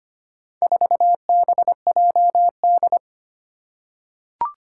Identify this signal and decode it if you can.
Morse code, heard as a keyed tone, about 25 wpm, 700 Hz. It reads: 4BJD
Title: Four Challenging Letter-Number Combinations 25wpm